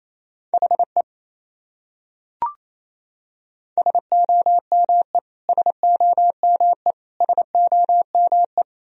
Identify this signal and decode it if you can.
Morse code, heard as a keyed tone, about 28 wpm, 700 Hz. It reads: HE SOME SOME SOME
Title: Top 100 Words Repeated 28wpm